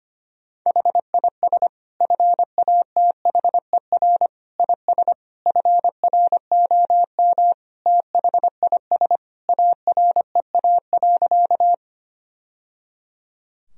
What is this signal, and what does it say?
Morse code, heard as a keyed tone, about 25 wpm, 700 Hz. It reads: HIS FATHER IS FROM THIS AREA.
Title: Sentences from Top 1000 Words - Review Part 2 - 25wpm